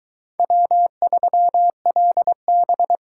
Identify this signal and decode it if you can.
Morse code, heard as a keyed tone, about 22 wpm, 700 Hz. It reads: W3LB